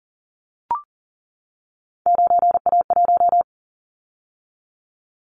Morse code, heard as a keyed tone, about 41 wpm, 700 Hz. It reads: 9A1
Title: Callsign Prefixes for Top 50 DXCC - 40wpm